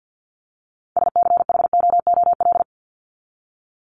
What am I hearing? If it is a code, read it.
Morse code, heard as a keyed tone, about 48 wpm, 700 Hz. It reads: SCHOOL
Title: Top 300 Words 50wpm